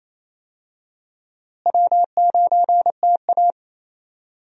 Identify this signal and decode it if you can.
Morse code, heard as a keyed tone, about 28 wpm, 700 Hz. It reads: W9TA
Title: USA Call Signs - 4 Characters - Repeated 3x - 28wpm